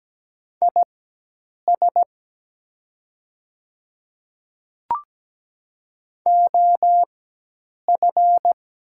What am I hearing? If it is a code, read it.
Morse code, heard as a keyed tone, about 17 wpm, spaced as for 8 wpm, 700 Hz. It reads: IS OF
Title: 4x Character Spacing - Top 100 Words - 17wpm